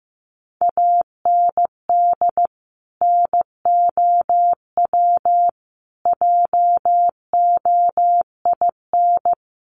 Morse code, heard as a keyed tone, about 15 wpm, 700 Hz. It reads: AND NOW JOIN
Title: Sentences from Top 1400 Words 15wpm